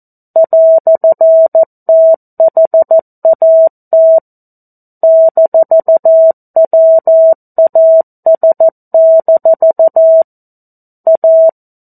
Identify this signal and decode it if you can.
Morse code, heard as a keyed tone, about 14 wpm, 650 Hz. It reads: "THAT -WAS- A
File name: 02-Chapter-02 14 WPM